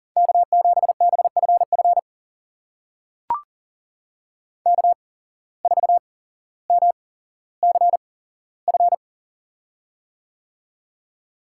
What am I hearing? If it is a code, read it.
Morse code, heard as a keyed tone, about 40 wpm, spaced as for 17 wpm, 700 Hz. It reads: K7BFF K4MCF
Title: USA Call Signs - 5 Characters - 8x Character Spacing - 40wpm